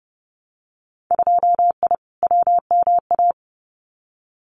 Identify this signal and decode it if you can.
Morse code, heard as a keyed tone, about 30 wpm, 700 Hz. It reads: 2I WMA
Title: Winter Field Day Exchanges Encore 30wpm